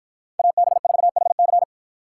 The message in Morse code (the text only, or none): AB4HC